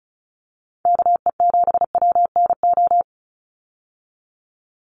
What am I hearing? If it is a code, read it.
Morse code, heard as a keyed tone, about 34 wpm, 700 Hz. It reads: KE7WNO
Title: USA Call Signs 35wpm